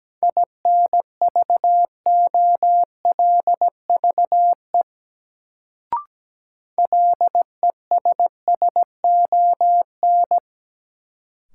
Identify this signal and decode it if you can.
Morse code, heard as a keyed tone, about 17 wpm, 700 Hz. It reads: INVOLVE LESSON
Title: Sentences from Top 1200 Words - New Word Drill - 17wpm